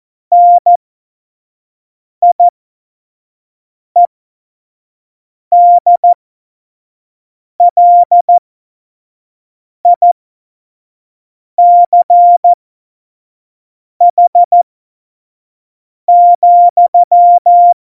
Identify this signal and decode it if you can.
Morse code, heard as a keyed tone, about 14 wpm, spaced as for 5 wpm, 700 Hz. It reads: NIEDLICH,
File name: Morsekurs / Witze / Witz 19. November 2019, 25 F70